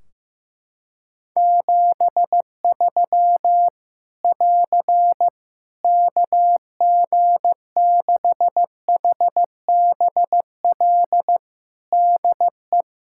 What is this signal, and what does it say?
Morse code, heard as a keyed tone, about 15 wpm, 700 Hz. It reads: 73 <AR> KG6HBL DE
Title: Practice QSO - Exchange 4 - Ending - 15wpm